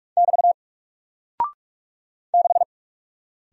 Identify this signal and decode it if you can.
Morse code, heard as a keyed tone, about 44 wpm, 700 Hz. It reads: <BT> 6
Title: Single Character (Letter:Number:Punctuation:Prosign) Alt 45wpm